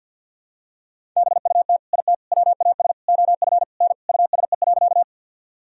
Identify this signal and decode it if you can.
Morse code, heard as a keyed tone, about 51 wpm, 700 Hz. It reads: BUT IT WAS OF N USE.
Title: Sentences from Top 100 Words and Abbreviations 50wpm